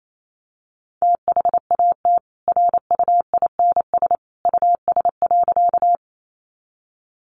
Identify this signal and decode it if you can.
Morse code, heard as a keyed tone, about 28 wpm, 700 Hz. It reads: THAT RUINS US.
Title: A Sound Of Thunder 28wpm